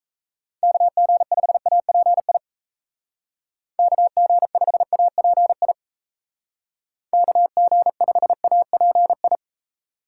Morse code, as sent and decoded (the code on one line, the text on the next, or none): -.- --. ..... .- .--. .. / -.- --. ..... .- .--. .. / -.- --. ..... .- .--. ..
KG5API KG5API KG5API